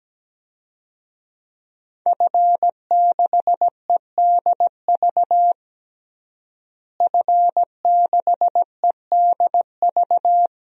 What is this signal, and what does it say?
Morse code, heard as a keyed tone, about 17 wpm, 700 Hz. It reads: F6EDV F6EDV
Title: DXCC-100 Call Signs - 5 Characters - Repeated 2x - 17wpm